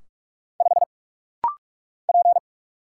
Morse code, heard as a keyed tone, about 44 wpm, 700 Hz. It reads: 5 P
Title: Single Letter-Number - Rapid-Fire 45wpm